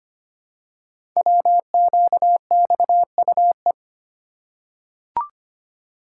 Morse code, heard as a keyed tone, about 25 wpm, 700 Hz. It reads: WQXUE
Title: Code Groups - 5 Letters - 25wpm